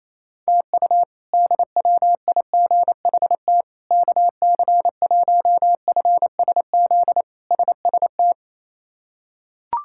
Morse code, heard as a keyed tone, about 28 wpm, 700 Hz. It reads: TU DWIGHT KC1FSZ SST
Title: K1USN SST QSO Practice - 28wpm